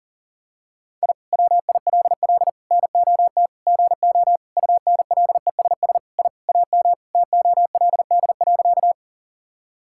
Morse code, heard as a keyed tone, about 37 wpm, 700 Hz. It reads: I WILL NOT GO UNLESS I AM TOLD.
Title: Sentences from Top 1000 Words - Review Part 6 - 40wpm